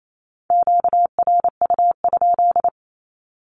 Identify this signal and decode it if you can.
Morse code, heard as a keyed tone, about 28 wpm, 700 Hz. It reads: QRU?
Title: Q-Codes Repeated 3x 28wpm